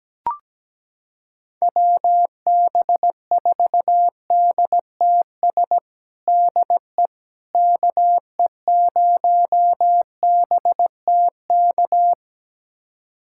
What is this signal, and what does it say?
Morse code, heard as a keyed tone, about 17 wpm, 700 Hz. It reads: WB4DTS DE KE0BTK
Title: USA Callsign DE Callsign - 6 Characters - 17wpm